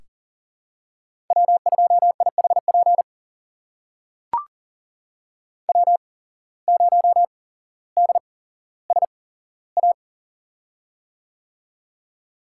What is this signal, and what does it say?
Morse code, heard as a keyed tone, about 38 wpm, spaced as for 17 wpm, 700 Hz. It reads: W2IHP W0DSA
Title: USA Call Signs - 5 Characters - 8x Character Spacing - 40wpm